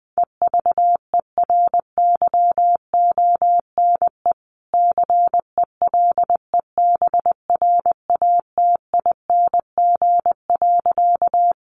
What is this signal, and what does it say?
Morse code, heard as a keyed tone, about 20 wpm, 700 Hz. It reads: EVERYONE CELEBRATING.
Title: A Sound Of Thunder 20wpm